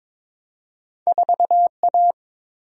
Morse code, heard as a keyed tone, about 22 wpm, 700 Hz. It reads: 4A
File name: practice / 22wpm / Two Letter-Number Combinations - 22wpm